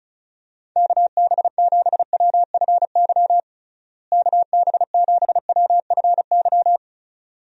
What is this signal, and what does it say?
Morse code, heard as a keyed tone, about 36 wpm, 700 Hz. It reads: KB7WFY KB7WFY
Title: USA Call Signs - 6 Characters - Repeated 2x - 35wpm